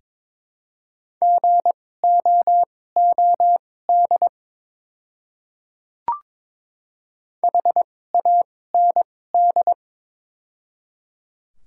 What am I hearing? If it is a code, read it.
Morse code, heard as a keyed tone, about 22 wpm, spaced as for 16 wpm, 700 Hz. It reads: GOOD HAND
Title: 2x Character Spacing - Top 200 Words - 22wpm